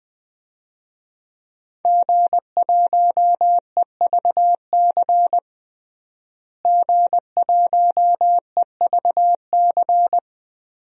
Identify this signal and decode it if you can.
Morse code, heard as a keyed tone, about 21 wpm, 700 Hz. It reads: G1EVC G1EVC
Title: DXCC-100 Call Signs - 5 Characters - Repeated 3x - 20wpm